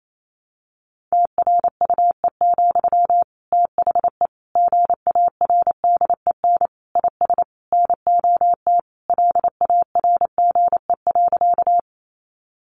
Morse code, heard as a keyed tone, about 28 wpm, 700 Hz. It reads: TRUE, THE GARDEN IS NOT LARGE.
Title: Sentences from Top 700 Words 28wpm